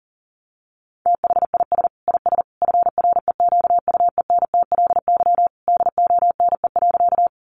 Morse code, heard as a keyed tone, about 40 wpm, 700 Hz. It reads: THIS IS FREQUENTLY DONE.
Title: Sentences from Top 1300 Words 40wpm